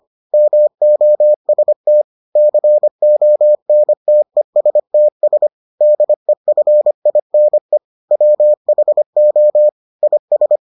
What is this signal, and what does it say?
Morse code, heard as a keyed tone, about 25 wpm, 600 Hz. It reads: MOST CONTESTS DEFINE WHO IS